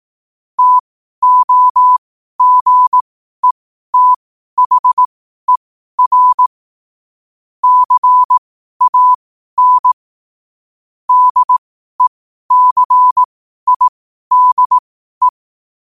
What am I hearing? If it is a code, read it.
Morse code, heard as a keyed tone, about 18 wpm, spaced as for 12 wpm, 1000 Hz. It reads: TOGETHER CAN DECIDE
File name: qotdfiles / 12wpm